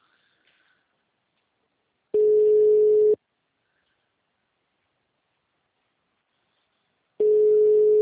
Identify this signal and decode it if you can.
Ringback tone (Europe, 425 Hz).
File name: out- -2003- - - .8049